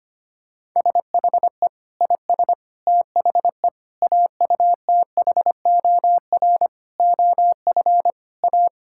Morse code, heard as a keyed tone, about 25 wpm, 700 Hz. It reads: SHE IS THE AUTHOR OF A
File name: practice / 25wpm / Sentences with 4-Word Phrases - 25wpm